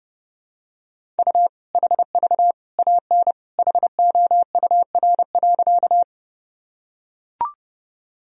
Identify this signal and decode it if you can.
Morse code, heard as a keyed tone, about 32 wpm, 700 Hz. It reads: U HV AN HOUR.
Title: Sentences from Top 400 Words and Abbreviations 30wpm